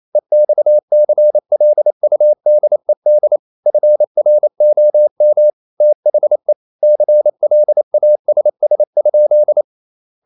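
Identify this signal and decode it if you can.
Morse code, heard as a keyed tone, about 28 wpm, 600 Hz. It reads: EXCLUDED FROM THE CLASS?
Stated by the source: Morse code practice at 28 wpm, text of Ken Follett's novel Never in English, from CW med Gnister.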